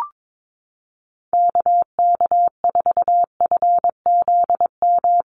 Morse code, heard as a keyed tone, about 22 wpm, 700 Hz. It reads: KK4FZM
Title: USA Call Signs Repeated 2x 22wpm